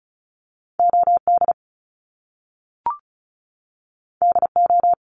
Morse code, heard as a keyed tone, about 36 wpm, 700 Hz. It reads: OD DO
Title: Top Two Letter Combinations in Words 35wpm